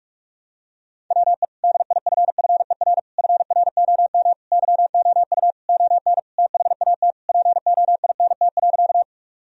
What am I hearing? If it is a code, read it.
Morse code, heard as a keyed tone, about 47 wpm, 700 Hz. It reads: WE DIFFER FROM YOU ON THAT POINT.
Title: Sentences from Top 1000 Words - Review Part 1 - 45wpm